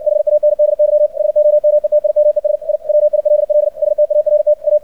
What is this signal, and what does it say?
Morse code, heard as a keyed tone, about 59 wpm, 600 Hz. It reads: HANDY SOMETIMES I WILL START A
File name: archived / CRSnet